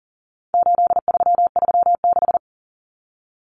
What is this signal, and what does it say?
Morse code, heard as a keyed tone, about 40 wpm, 700 Hz. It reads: 8336
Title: Numbers - 4-Digit 40wpm